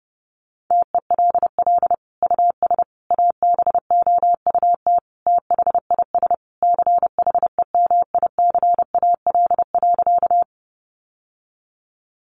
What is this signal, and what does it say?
Morse code, heard as a keyed tone, about 30 wpm, 700 Hz. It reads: TELL US ABOUT THIS CHEMICAL.
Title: Sentences from Top 1200 Words 30wpm